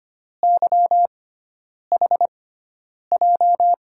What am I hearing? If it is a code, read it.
Morse code, heard as a keyed tone, about 25 wpm, spaced as for 9 wpm, 700 Hz. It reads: YHJ
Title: USA Call Signs - 6 Characters - 6x Character Spacing - 25wpm